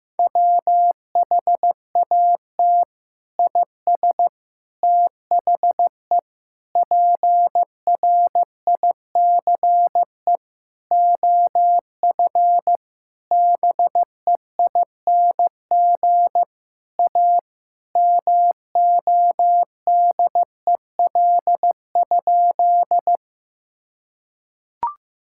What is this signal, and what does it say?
Morse code, heard as a keyed tone, about 15 wpm, 700 Hz. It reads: WHAT IS THE PRICE OF BEING A MODEL?
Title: Sentences from Top 1000 Words - Review Part 5 - 15wpm